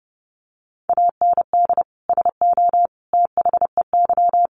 Morse code, heard as a keyed tone, about 30 wpm, 700 Hz. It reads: AND SO THEY
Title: Sentences from Top 400 Words 30wpm